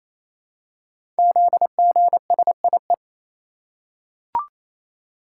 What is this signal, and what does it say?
Morse code, heard as a keyed tone, about 27 wpm, 700 Hz. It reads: ZGSIE
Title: Code Groups - 5 Letters - 28wpm